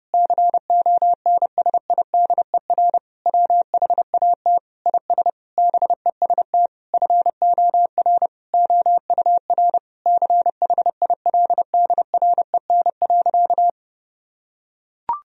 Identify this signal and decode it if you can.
Morse code, heard as a keyed tone, about 30 wpm, 700 Hz. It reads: CONSIDER WHAT IS BEST FOR OUR CHILDREN.
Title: Sentences from Top 1000 Words - Review Part 4 - 30wpm